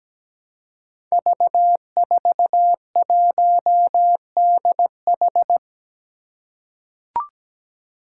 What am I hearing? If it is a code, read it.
Morse code, heard as a keyed tone, about 17 wpm, 700 Hz. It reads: V41DH
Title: Five Challenging Letter-Number Combinations 17wpm